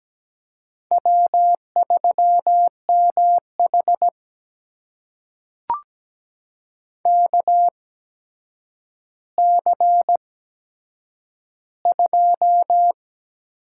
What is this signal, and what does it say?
Morse code, heard as a keyed tone, about 17 wpm, spaced as for 8 wpm, 700 Hz. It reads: W3MH KC2
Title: USA Call Signs - 8x Character Spacing - 17wpm